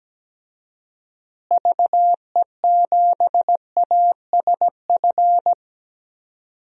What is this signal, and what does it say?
Morse code, heard as a keyed tone, about 17 wpm, 700 Hz. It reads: VE7ASF
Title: DXCC-100 Call Signs - 6 Characters - Repeated 3x - 17wpm